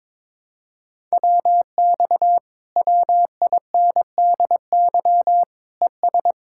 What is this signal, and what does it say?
Morse code, heard as a keyed tone, about 22 wpm, 700 Hz. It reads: WX WINDY ES
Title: Speed-Racing Common QSO Phrases - 22wpm